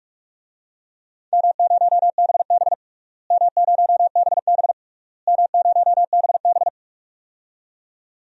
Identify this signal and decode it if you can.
Morse code, heard as a keyed tone, about 46 wpm, 700 Hz. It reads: M0BB M0BB M0BB
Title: DXCC-100 Call Signs - 4 Characters - Repeated 3x - 45wpm